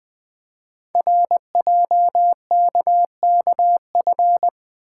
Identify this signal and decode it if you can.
Morse code, heard as a keyed tone, about 20 wpm, 700 Hz. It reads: RJKKF
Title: Code Groups - 5 Letters - 20wpm